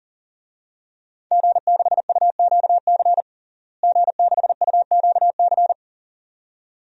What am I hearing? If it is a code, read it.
Morse code, heard as a keyed tone, about 40 wpm, 700 Hz. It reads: G6UQC G6UQC
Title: DXCC-100 Call Signs - 5 Characters - Repeated 2x - 40wpm